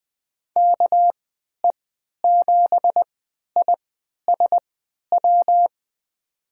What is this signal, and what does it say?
Morse code, heard as a keyed tone, about 20 wpm, spaced as for 11 wpm, 700 Hz. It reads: KE7ISW
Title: USA Call Signs - 6 Characters - 3x Character Spacing - 20wpm